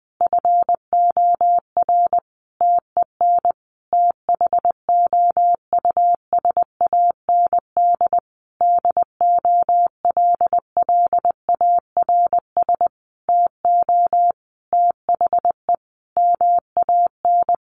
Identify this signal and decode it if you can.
Morse code, heard as a keyed tone, about 20 wpm, 700 Hz. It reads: FOR TEN THOUSAND DOLLARS TO THE MAN
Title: A Sound Of Thunder 20wpm